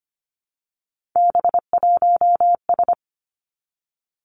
Morse code, heard as a keyed tone, about 25 wpm, 700 Hz. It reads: B1S
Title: Three Challenging Letter-Number Combinations 25wpm